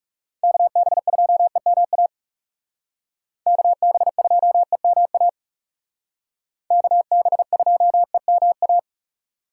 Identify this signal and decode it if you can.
Morse code, heard as a keyed tone, about 38 wpm, 700 Hz. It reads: KB2EMA KB2EMA KB2EMA